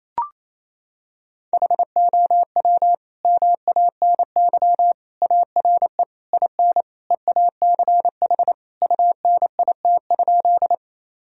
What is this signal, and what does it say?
Morse code, heard as a keyed tone, about 28 wpm, 700 Hz. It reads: HOW MANY ARE IN EACH UNIT?
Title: Sentences from Top 1000 Words - Review Part 2 - 28wpm